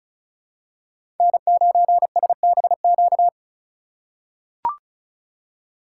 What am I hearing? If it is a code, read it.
Morse code, heard as a keyed tone, about 34 wpm, 700 Hz. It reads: N9SBQ
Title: USA Call Signs - 5 Characters - 2x Character Spacing - 35wpm